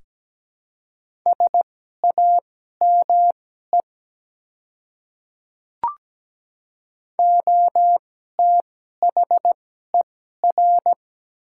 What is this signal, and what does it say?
Morse code, heard as a keyed tone, about 17 wpm, spaced as for 12 wpm, 700 Hz. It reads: SAME OTHER
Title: 2x Character Spacing - Top 200 Words - 17wpm